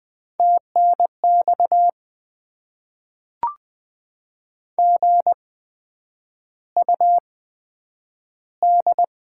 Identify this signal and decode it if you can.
Morse code, heard as a keyed tone, about 20 wpm, spaced as for 9 wpm, 700 Hz. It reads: TNX GUD
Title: Top 20 QSO Elements - 8x Character Spacing - 20wpm